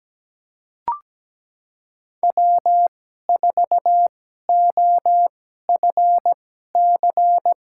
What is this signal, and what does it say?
Morse code, heard as a keyed tone, about 17 wpm, spaced as for 12 wpm, 700 Hz. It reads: W4OFC
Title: USA Call Signs - 5 Characters - 2x Character Spacing - 17wpm